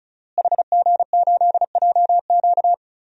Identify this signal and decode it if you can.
Morse code, heard as a keyed tone, about 36 wpm, 700 Hz. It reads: HG8JQ
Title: DXCC-100 Call Signs - 5 Characters - Repeated 3x - 35wpm